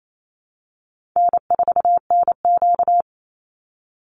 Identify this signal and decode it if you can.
Morse code, heard as a keyed tone, about 28 wpm, 700 Hz. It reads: N4NQ
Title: Easy Call Signs Repeated 3x - 28wpm